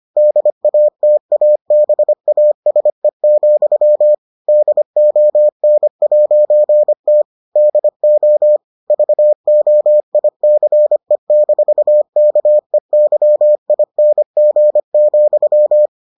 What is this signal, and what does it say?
Morse code, heard as a keyed tone, about 25 wpm, 600 Hz. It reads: DATABASE, DON'T DO VOICE-KEYING,